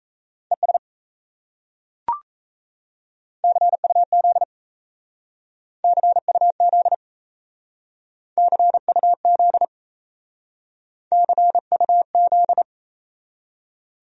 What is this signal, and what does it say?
Morse code, heard as a keyed tone, about 37 wpm, 700 Hz. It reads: ES CUZ CUZ CUZ CUZ